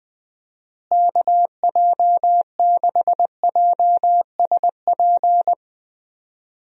Morse code, heard as a keyed tone, about 20 wpm, 700 Hz. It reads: KJ6JSP
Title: USA Call Signs - 6 Characters - Repeated 2x - 20wpm